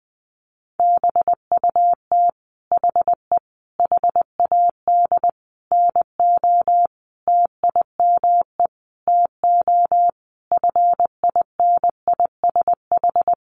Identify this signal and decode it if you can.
Morse code, heard as a keyed tone, about 20 wpm, 700 Hz. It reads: BUT HE HAD NO TIME TO FINISH
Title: Sentences from Top 1200 Words 20wpm